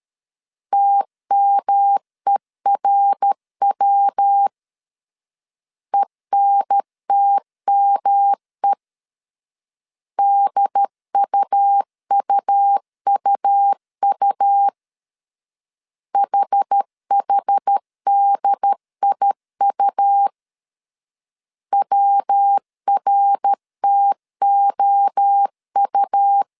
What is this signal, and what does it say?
Morse code, heard as a keyed tone, about 13 wpm, 800 Hz. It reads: TMERW ENTME DUUUU HHDIU WRTOU
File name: CW / DET / 1 3 1500C S C 12wpm 2750